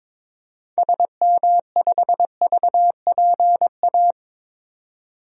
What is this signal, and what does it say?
Morse code, heard as a keyed tone, about 22 wpm, 700 Hz. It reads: SM5VPA
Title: DXCC-100 Call Signs - 6 Characters - 22wpm